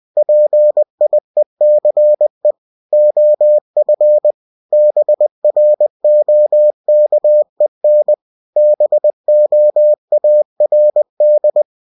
Morse code, heard as a keyed tone, about 20 wpm, 600 Hz. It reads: PIECE OF BROKEN BOARD
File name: cw-books / winnie-the-pooh / winnie-20wpm-page-0020